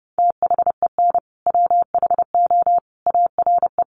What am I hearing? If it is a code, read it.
Morse code, heard as a keyed tone, about 30 wpm, 700 Hz. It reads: THEN WHO ARE
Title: Sentences from Top 100 Words 30wpm